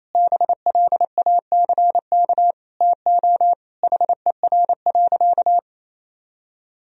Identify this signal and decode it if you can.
Morse code, heard as a keyed tone, about 28 wpm, 700 Hz. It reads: BLACK TO HER.
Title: Sentences from Top 1000 Words - Review Part 2 - 28wpm